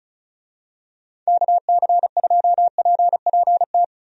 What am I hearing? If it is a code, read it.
Morse code, heard as a keyed tone, about 34 wpm, 700 Hz. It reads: KC2PPT
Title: USA Call Signs - 6 Characters - 35wpm